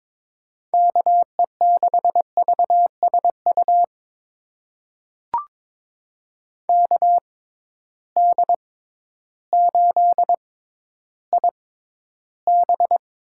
Morse code, heard as a keyed tone, about 21 wpm, spaced as for 12 wpm, 700 Hz. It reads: KE6VSU KD8IB
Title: USA Call Signs - 6 Characters - 6x Character Spacing - 22wpm